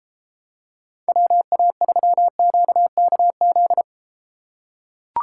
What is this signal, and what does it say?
Morse code, heard as a keyed tone, about 33 wpm, 700 Hz. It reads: WA3QKZ